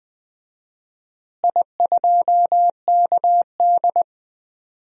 Morse code, heard as a keyed tone, about 20 wpm, 700 Hz. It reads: I2KD
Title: DXCC-100 Call Signs - 4 Characters - Repeated 2x - 20wpm